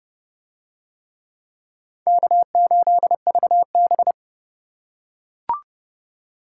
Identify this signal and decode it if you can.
Morse code, heard as a keyed tone, about 27 wpm, 700 Hz. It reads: K8VB